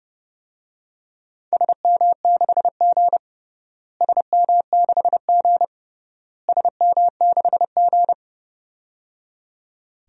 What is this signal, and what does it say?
Morse code, heard as a keyed tone, about 30 wpm, 700 Hz. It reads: SM6G SM6G SM6G
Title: DXCC-100 Call Signs - 4 Characters - Repeated 3x - 30wpm